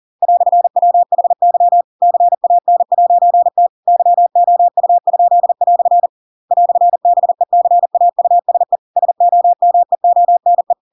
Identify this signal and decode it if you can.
Morse code, heard as a keyed tone, about 40 wpm, 700 Hz. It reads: "WHY CAN'T YOU?" "BECAUSE SOMEONE
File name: Pinocchio / Pinocchio ch7 40 40 ik1hge